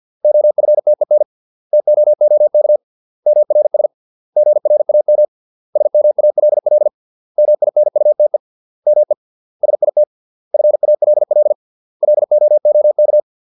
Morse code, heard as a keyed tone, about 50 wpm, 600 Hz. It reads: OFTEN TOOK MRS GRAM SMALL MINUTE ME SIT FALL LOOK